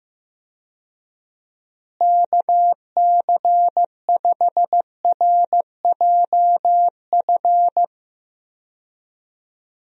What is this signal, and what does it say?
Morse code, heard as a keyed tone, about 16 wpm, 700 Hz. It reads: KC5RJF
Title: USA Call Signs - 6 Characters - Repeated 2x - 15wpm